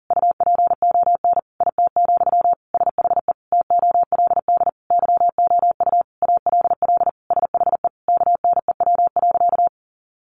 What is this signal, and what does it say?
Morse code, heard as a keyed tone, about 40 wpm, 700 Hz. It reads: UPON IT, SHE TOLD YOU ALL SHE KNEW.
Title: Sentences from Top 1300 Words 40wpm